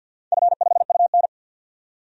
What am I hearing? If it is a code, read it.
Morse code, heard as a keyed tone, about 49 wpm, 700 Hz. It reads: F5UN
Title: DXCC-100 Call Signs - 4 Characters - 50wpm